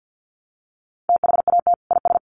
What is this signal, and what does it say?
Morse code, heard as a keyed tone, about 50 wpm, 700 Hz. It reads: THAT IS
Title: Sentences from Top 1200 Words 50wpm